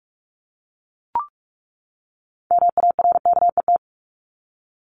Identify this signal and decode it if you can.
Morse code, heard as a keyed tone, about 44 wpm, 700 Hz. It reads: MARKET
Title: ING Suffix - Top 100 Words with Base Words 45wpm